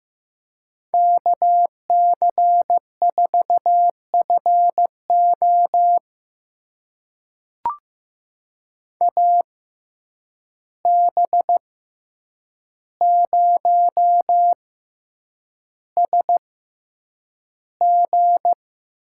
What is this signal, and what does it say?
Morse code, heard as a keyed tone, about 15 wpm, spaced as for 8 wpm, 700 Hz. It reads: KC4FO AB0SG
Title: USA Call Signs - 5 Characters - 6x Character Spacing - 15wpm